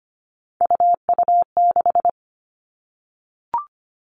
Morse code, heard as a keyed tone, about 25 wpm, 700 Hz. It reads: UU6
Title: All Cut Numbers - 3-Digit 25wpm